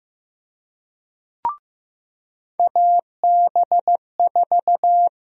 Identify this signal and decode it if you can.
Morse code, heard as a keyed tone, about 15 wpm, 700 Hz. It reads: AB4
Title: USA Call Sign Prefixes - Repeated 2x - 15wpm